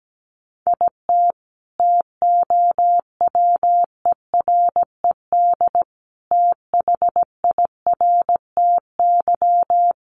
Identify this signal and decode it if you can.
Morse code, heard as a keyed tone, about 17 wpm, 700 Hz. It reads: IT TOWERED THIRTY
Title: A Sound Of Thunder 17wpm